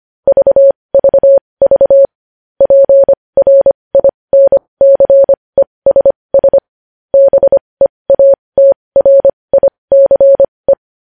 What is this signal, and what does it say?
Morse code, heard as a keyed tone, about 25 wpm, spaced as for 20 wpm, 550 Hz. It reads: VVV PRINCESS BEATRICE
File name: podcast / News headlines-1x-25x20